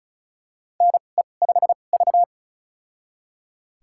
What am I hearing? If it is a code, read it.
Morse code, heard as a keyed tone, about 34 wpm, spaced as for 25 wpm, 700 Hz. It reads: NE5V
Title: USA Call Signs - 4 Characters - 2x Character Spacing - 35wpm